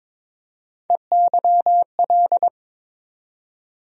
Morse code, heard as a keyed tone, about 22 wpm, 700 Hz. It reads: EYL
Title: USA Call Sign Postfixes - 3 Letters 22wpm